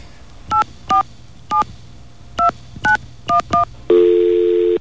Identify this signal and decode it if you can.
DTMF, heard as telephone touch tones, tones of about 120 ms. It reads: 8472911
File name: mp3files / off the hook -128